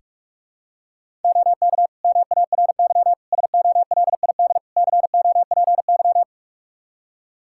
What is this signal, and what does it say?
Morse code, heard as a keyed tone, about 46 wpm, 700 Hz. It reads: OK MARY SOLID COPY
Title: Practice QSO - Exchange 3 - More Info - 45wpm